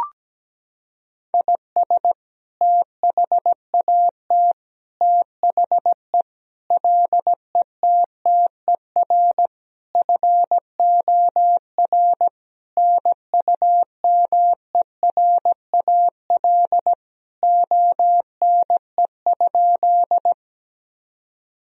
Morse code, heard as a keyed tone, about 17 wpm, 700 Hz. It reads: IS THAT THE LETTER FOR NUMERAL ONE?
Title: Sentences from Top 1000 Words - Review Part 2 - 17wpm